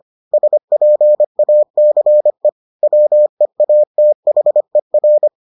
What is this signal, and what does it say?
Morse code, heard as a keyed tone, about 25 wpm, 600 Hz. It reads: SPACE WEATHER